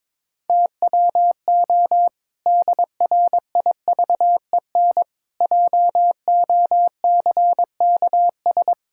Morse code, heard as a keyed tone, about 22 wpm, 700 Hz. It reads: TWO DRIVEN JOCKS
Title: Pangram Challenge v2 - 22wpm